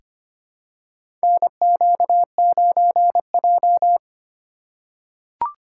Morse code, heard as a keyed tone, about 25 wpm, 700 Hz. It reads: NQ9J
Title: USA Call Signs - 2x Character Spacing - 25wpm